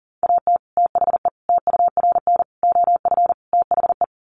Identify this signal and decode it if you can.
Morse code, heard as a keyed tone, about 40 wpm, 700 Hz. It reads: AT THE TURN OF THE
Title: Sets of 5 Words 40wpm